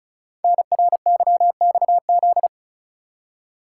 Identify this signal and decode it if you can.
Morse code, heard as a keyed tone, about 35 wpm, 700 Hz. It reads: NRYXZ
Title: Code Groups - 5 Letters - 35wpm